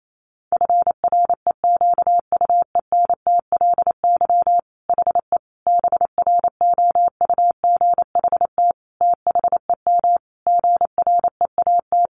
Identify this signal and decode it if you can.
Morse code, heard as a keyed tone, about 28 wpm, 700 Hz. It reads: FREQUENTLY HE BROUGHT THEM GREAT
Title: Sentences from Top 1300 Words 28wpm